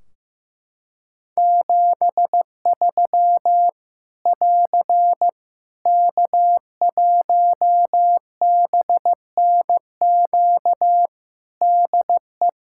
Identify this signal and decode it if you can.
Morse code, heard as a keyed tone, about 15 wpm, 700 Hz. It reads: 73 <AR> K1BNQ DE
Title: Practice QSO - Exchange 4 - Ending - 15wpm